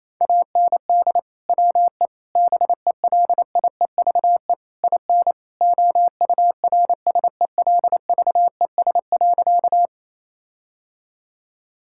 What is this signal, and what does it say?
Morse code, heard as a keyed tone, about 27 wpm, 700 Hz. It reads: AND WE BELIEVE IN OURSELVES.
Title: Sentences from Top 1600 Words 28wpm